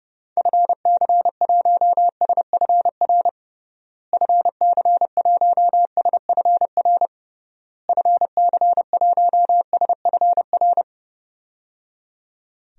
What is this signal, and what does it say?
Morse code, heard as a keyed tone, about 30 wpm, 700 Hz. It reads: FC1SFR FC1SFR FC1SFR
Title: DXCC-100 Call Signs - 6 Characters - Repeated 3x - 30wpm